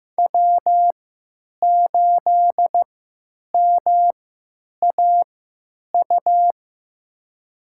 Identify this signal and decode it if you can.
Morse code, heard as a keyed tone, about 15 wpm, spaced as for 9 wpm, 700 Hz. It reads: W8MAU
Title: USA Call Signs - 3x Character Spacing - 15wpm